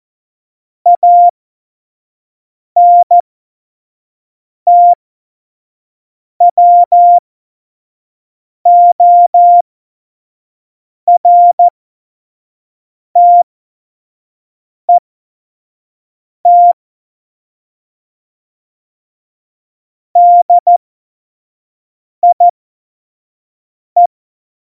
Morse code, heard as a keyed tone, about 14 wpm, spaced as for 5 wpm, 700 Hz. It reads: ANTWORTET DIE
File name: Morsekurs / Witze / Witz 13. November 2018, 25 F70